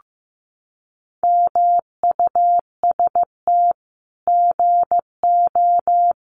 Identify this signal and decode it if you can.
Morse code, heard as a keyed tone, about 15 wpm, 700 Hz. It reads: MUST GO
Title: Common QSO Phrases 15wpm